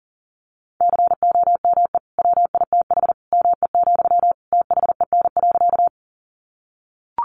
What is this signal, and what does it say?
Morse code, heard as a keyed tone, about 40 wpm, 700 Hz. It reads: COME WITH ME, THEN.
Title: Sentences from Top 200 Words 40wpm